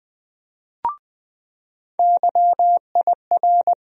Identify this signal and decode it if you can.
Morse code, heard as a keyed tone, about 21 wpm, 700 Hz. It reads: YIR
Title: USA Call Sign Postfixes - 3 Letters 20wpm